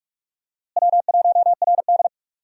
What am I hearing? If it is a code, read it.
Morse code, heard as a keyed tone, about 41 wpm, 700 Hz. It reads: W1RD
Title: USA Call Signs - 4 Characters - Repeated 2x - 45wpm